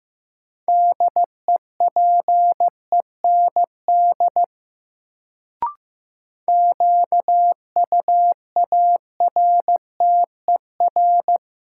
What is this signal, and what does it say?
Morse code, heard as a keyed tone, about 15 wpm, 700 Hz. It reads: DEPEND QUARTER
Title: Sentences from Top 1300 Words - New Word Drill Alt - 15wpm